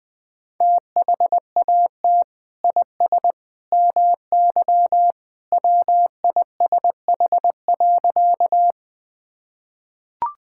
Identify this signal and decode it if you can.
Morse code, heard as a keyed tone, about 20 wpm, 700 Hz. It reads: THAT IS MY WISH.
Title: Sentences from Top 1600 Words 20wpm